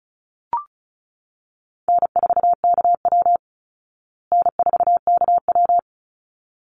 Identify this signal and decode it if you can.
Morse code, heard as a keyed tone, about 34 wpm, 700 Hz. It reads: N4KW N4KW
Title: Call Signs Repeated 3x - 35wpm